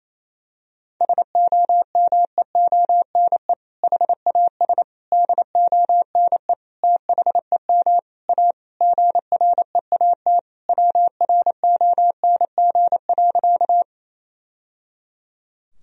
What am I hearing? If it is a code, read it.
Morse code, heard as a keyed tone, about 27 wpm, 700 Hz. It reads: SOMEONE HAS DONE THEM A GREAT WRONG.
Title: Sentences from Top 1000 Words - Review Part 4 - 28wpm